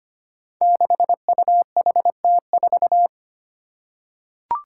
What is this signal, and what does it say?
Morse code, heard as a keyed tone, about 25 wpm, 700 Hz. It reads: 6UHT4
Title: Five Challenging Letter-Number Combinations 25wpm